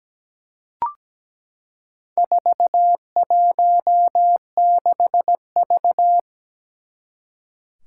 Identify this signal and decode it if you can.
Morse code, heard as a keyed tone, about 17 wpm, 700 Hz. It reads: 416V
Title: Four Challenging Letter-Number Combinations 17wpm